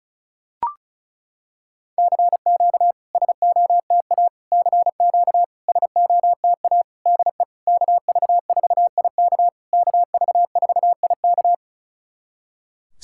Morse code, heard as a keyed tone, about 36 wpm, 690 Hz. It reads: CQ SOTA CQ SOTA DE KV4IK KV4IK
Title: SOTA QSO Practice 35wpm